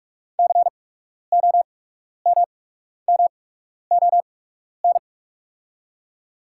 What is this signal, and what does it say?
Morse code, heard as a keyed tone, about 44 wpm, spaced as for 12 wpm, 700 Hz. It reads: COMMON
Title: 8x Character Spacing - Top 500 Words - 45wpm